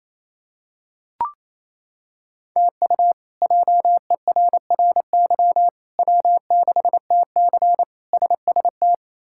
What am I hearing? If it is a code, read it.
Morse code, heard as a keyed tone, about 27 wpm, 700 Hz. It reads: TU JERRY W6TC SST
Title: K1USN SST QSO Practice - 28wpm